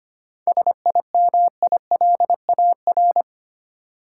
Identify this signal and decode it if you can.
Morse code, heard as a keyed tone, about 25 wpm, 700 Hz. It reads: SIMILAR
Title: Seven Letter Words - 25wpm